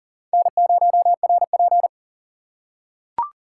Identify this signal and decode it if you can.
Morse code, heard as a keyed tone, about 40 wpm, 700 Hz. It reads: N0RP
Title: USA Call Signs - 4 Characters - 3x Character Spacing - 40wpm